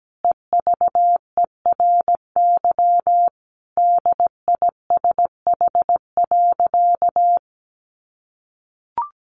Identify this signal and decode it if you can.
Morse code, heard as a keyed tone, about 17 wpm, 700 Hz. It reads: EVERY DISH.
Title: Sentences from Top 1300 Words 17wpm